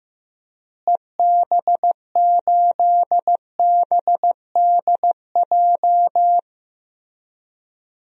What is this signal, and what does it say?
Morse code, heard as a keyed tone, about 16 wpm, 700 Hz. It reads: EB8BDJ
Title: DXCC-100 Call Signs - 6 Characters - Repeated 3x - 15wpm